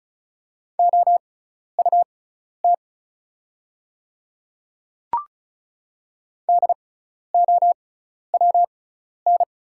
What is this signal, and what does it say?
Morse code, heard as a keyed tone, about 35 wpm, spaced as for 12 wpm, 690 Hz. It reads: OUT DOWN
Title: 6x Character Spacing - Top 100 Words - 35wpm